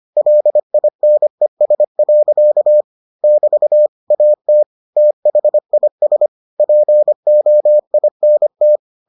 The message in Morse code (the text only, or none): LINES. <BT> AT THIS POINT